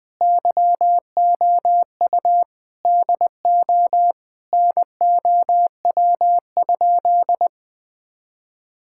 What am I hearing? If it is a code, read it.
Morse code, heard as a keyed tone, about 21 wpm, 700 Hz. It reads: YOU DO NOW?
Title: Sentences with 2-Word Phrases - Spoken Beginning - 20wpm